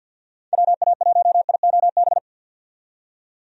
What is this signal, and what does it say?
Morse code, heard as a keyed tone, about 49 wpm, 700 Hz. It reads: WA1IOB
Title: USA Call Signs - 6 Characters - 50wpm